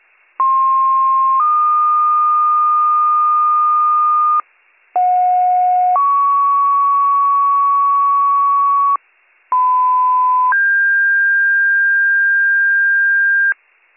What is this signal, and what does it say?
Two-tone page, heard as a one-second tone, then a three-second tone, about 1060 then 1220 Hz, about 730 then 1120 Hz, about 1010 then 1600 Hz. It is heard from 0.4 to 13.5 s.